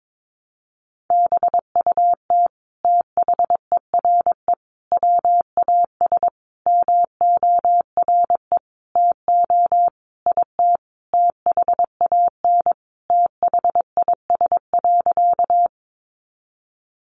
Morse code, heard as a keyed tone, about 22 wpm, 700 Hz. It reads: BUT THERE WAS MORE TO IT THAN THIS.
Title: Sentences from Top 100 Words 22wpm